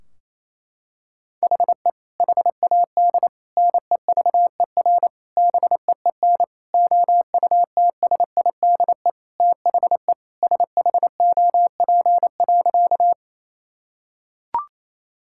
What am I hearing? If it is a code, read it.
Morse code, heard as a keyed tone, about 28 wpm, 700 Hz. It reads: HE HAD NEVER BEEN OUTSIDE THE SHOP.
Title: Sentences from Top 1600 Words 28wpm